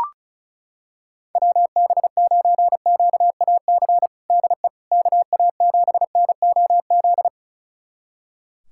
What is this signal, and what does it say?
Morse code, heard as a keyed tone, about 36 wpm, 700 Hz. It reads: WB9QAC DE KA7NOZ
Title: USA Callsign DE Callsign - 6 Characters - 35wpm